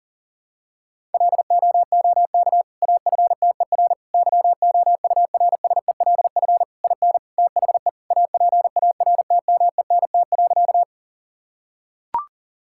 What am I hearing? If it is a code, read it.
Morse code, heard as a keyed tone, about 40 wpm, 700 Hz. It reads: LOOK AFTER YOURSELF IN THE APARTMENT.
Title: Sentences from Top 1000 Words - Review Part 6 - 40wpm